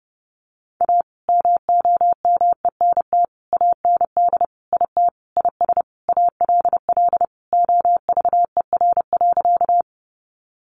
Morse code, heard as a keyed tone, about 30 wpm, 700 Hz. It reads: A MOMENT AND IT IS ALL OVER.
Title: Sentences from Top 1000 Words 30wpm